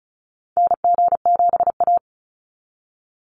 Morse code, heard as a keyed tone, about 34 wpm, 700 Hz. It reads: NG7A
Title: NA SOTA Activator Callsigns 35wpm